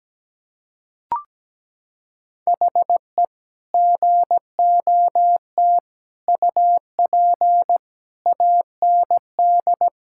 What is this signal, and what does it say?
Morse code, heard as a keyed tone, about 17 wpm, 700 Hz. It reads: HE GOT UP AND
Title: Sentences from Top 1600 Words 17wpm